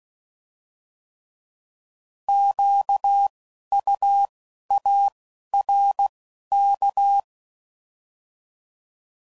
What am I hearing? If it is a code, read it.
Morse code, heard as a keyed tone, about 16 wpm, 800 Hz. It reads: Q U A R K
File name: Prefixes / By First Letter / Qs